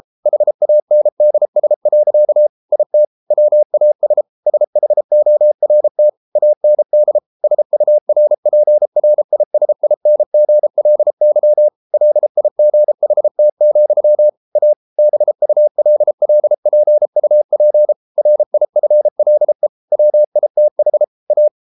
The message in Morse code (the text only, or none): HANDS. IT WAS SHORT AND SURPRISINGLY LIGHT, A BULLPUP RIFLE WITH A